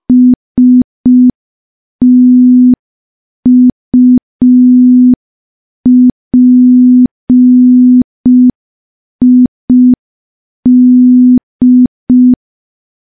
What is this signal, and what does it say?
Morse code, heard as a keyed tone, about 5 wpm, 250 Hz. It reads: STUPID
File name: morse / morse-training-05WPM